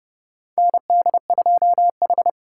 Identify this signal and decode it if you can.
Morse code, heard as a keyed tone, about 30 wpm, 700 Hz. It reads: ND2H
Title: USA Call Signs - 4 Characters - 2x Character Spacing - 30wpm